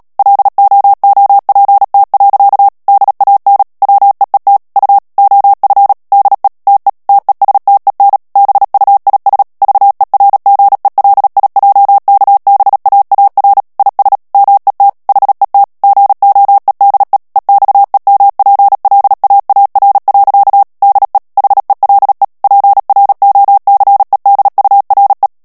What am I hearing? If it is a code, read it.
Morse code, heard as a keyed tone, about 37 wpm, 800 Hz. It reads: LOOPT. DAN WEET U OF DE TE TESTEN BUIS VERGELIJKBAAR IS MET HET GOEDE EXEMPLAAR. DE HELE PROCEDURE
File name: morse mp3 / HOLLANDAIS texte clair 2 vitesse 35 mots minute